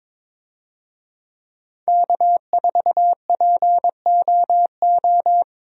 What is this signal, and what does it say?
Morse code, heard as a keyed tone, about 21 wpm, 700 Hz. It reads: K4POO
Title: USA Call Signs - 5 Characters - 22wpm